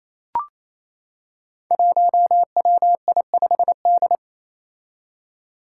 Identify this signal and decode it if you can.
Morse code, heard as a keyed tone, about 29 wpm, 700 Hz. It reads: 1WI5D
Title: Five Challenging Letter-Number Combinations 28wpm